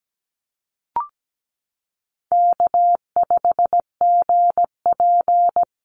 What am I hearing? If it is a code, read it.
Morse code, heard as a keyed tone, about 17 wpm, 700 Hz. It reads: K5GP
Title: Call Signs Repeated 3x - 17wpm